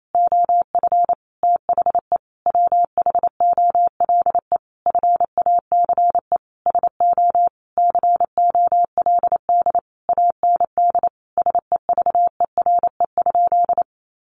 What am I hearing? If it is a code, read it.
Morse code, heard as a keyed tone, about 28 wpm, 700 Hz. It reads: OF THE WHOLE FACE SO COLD AND SEVERE?
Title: Sentences from Top 1400 Words 28wpm